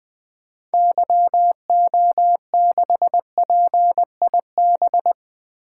Morse code, heard as a keyed tone, about 20 wpm, 700 Hz. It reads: YO6PIB
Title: 2023 SOTA Activator Call Signs - 20wpm